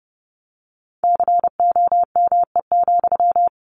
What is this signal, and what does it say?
Morse code, heard as a keyed tone, about 30 wpm, 700 Hz. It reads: COME,
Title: Sentences from Top 1500 Words 30wpm